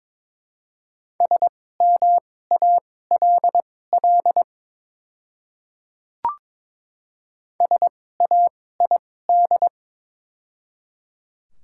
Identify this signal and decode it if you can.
Morse code, heard as a keyed tone, about 22 wpm, spaced as for 16 wpm, 700 Hz. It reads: SMALL SAID
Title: 2x Character Spacing - Top 200 Words - 22wpm